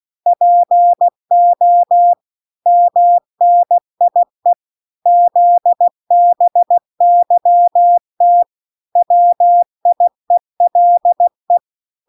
Morse code, heard as a keyed tone, about 16 wpm, 700 Hz. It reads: PO MNIE ZBYT WIELE